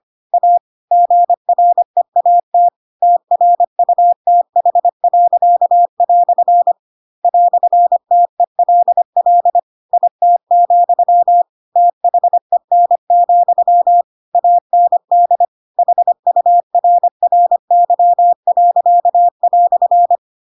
Morse code, heard as a keyed tone, about 25 wpm, 700 Hz. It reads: A GREAT TRUTH." "TELL IT, THEN, AND HURRY."
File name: Pinocchio / Pinocchio ch4 25 25 ik1hge